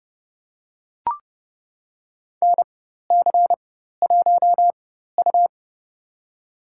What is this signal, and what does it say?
Morse code, heard as a keyed tone, about 30 wpm, spaced as for 14 wpm, 700 Hz. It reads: NC1U
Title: USA Call Signs - 4 Characters - 4x Character Spacing - 30wpm